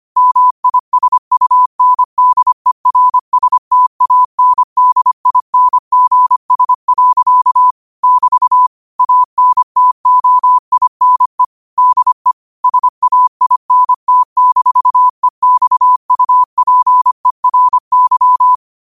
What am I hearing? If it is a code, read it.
Morse code, heard as a keyed tone, about 25 wpm, 1000 Hz. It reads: MISUNDERSTANDINGS. <BT> ANTOINE DE SAINT-EXUPERY